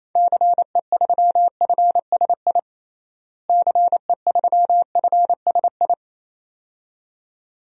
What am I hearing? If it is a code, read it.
Morse code, heard as a keyed tone, about 28 wpm, 700 Hz. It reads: CE3FSI CE3FSI
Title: DXCC-100 Call Signs - 6 Characters - Repeated 2x - 28wpm